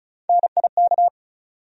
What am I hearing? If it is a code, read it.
Morse code, heard as a keyed tone, about 36 wpm, 700 Hz. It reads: NIK
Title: USA Call Sign Postfixes - 3 Letters 35wpm